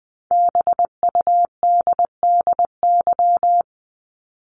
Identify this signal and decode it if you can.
Morse code, heard as a keyed tone, about 20 wpm, 700 Hz. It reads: BUDDY
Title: Top 1751-2000 Words 20wpm